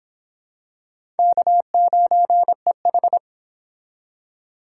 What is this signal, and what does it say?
Morse code, heard as a keyed tone, about 25 wpm, 700 Hz. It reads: K9EH